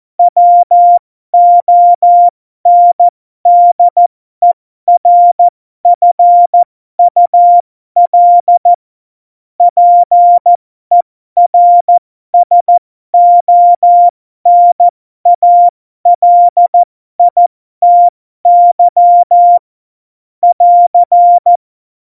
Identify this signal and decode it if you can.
Morse code, heard as a keyed tone, about 14 wpm, spaced as for 12 wpm, 700 Hz. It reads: WONDERFUL PERSONALITY <AR>
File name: Morsekurs / Witze